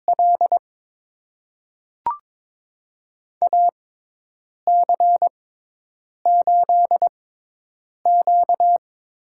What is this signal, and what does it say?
Morse code, heard as a keyed tone, about 22 wpm, spaced as for 8 wpm, 700 Hz. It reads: L AC8Q
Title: USA Call Signs - 5 Characters - 6x Character Spacing - 22wpm